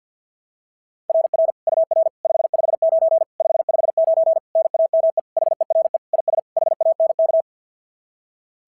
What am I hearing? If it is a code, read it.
Morse code, heard as a keyed tone, about 51 wpm, 690 Hz. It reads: RR UR 559 559 NAME HERE IS HANK
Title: SOTA QSO Practice 50wpm